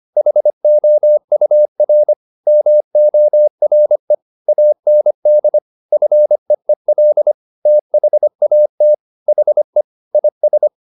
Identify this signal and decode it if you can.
Morse code, heard as a keyed tone, about 25 wpm, 600 Hz. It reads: HOUR MORE AND FEEL THAT HE IS